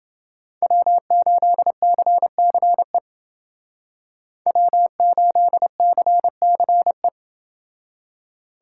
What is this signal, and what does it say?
Morse code, heard as a keyed tone, about 28 wpm, 700 Hz. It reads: W8CCE W8CCE